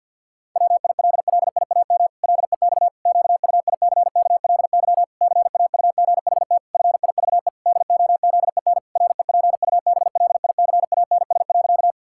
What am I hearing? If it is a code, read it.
Morse code, heard as a keyed tone, about 50 wpm, 700 Hz. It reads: WILLIAM LEX QUICKLY CAUGHT FIVE DOZEN REPUBLICANS.
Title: Pangram Challenge v2 - 50wpm